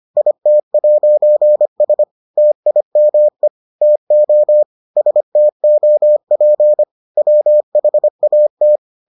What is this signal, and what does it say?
Morse code, heard as a keyed tone, about 25 wpm, 600 Hz. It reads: IT'S TIME TO STOP WHAT